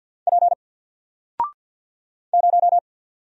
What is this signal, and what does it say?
Morse code, heard as a keyed tone, about 50 wpm, 700 Hz. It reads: P 0
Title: Single Letter-Number 50wpm